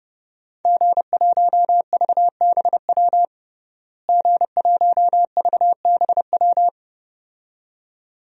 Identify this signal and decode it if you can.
Morse code, heard as a keyed tone, about 32 wpm, 700 Hz. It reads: G1VBW G1VBW
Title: DXCC-100 Call Signs - 5 Characters - Repeated 2x - 30wpm